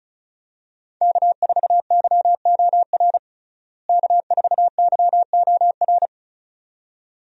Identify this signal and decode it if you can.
Morse code, heard as a keyed tone, about 34 wpm, 700 Hz. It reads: K4YOR K4YOR
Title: USA Call Signs - 5 Characters - Repeated 2x - 35wpm